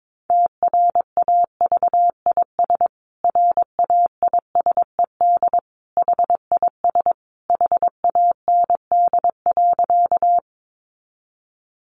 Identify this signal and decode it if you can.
Morse code, heard as a keyed tone, about 22 wpm, 700 Hz. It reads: TRAVIS RAISED HIS HAND.
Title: A Sound Of Thunder 22wpm